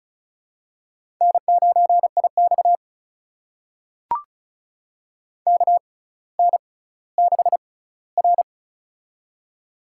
Morse code, heard as a keyed tone, about 36 wpm, spaced as for 18 wpm, 700 Hz. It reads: N9IX KN6R
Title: USA Call Signs - 4 Characters - 6x Character Spacing - 35wpm